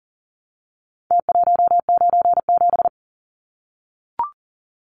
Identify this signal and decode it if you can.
Morse code, heard as a keyed tone, about 40 wpm, 700 Hz. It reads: T197
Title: Numbers - Contest Style - 4-Digit 40wpm